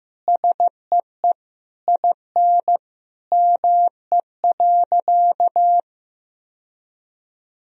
Morse code, heard as a keyed tone, about 15 wpm, 700 Hz. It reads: SEE IN ME.
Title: Sentences from Top 1000 Words - Review Part 3 - 15wpm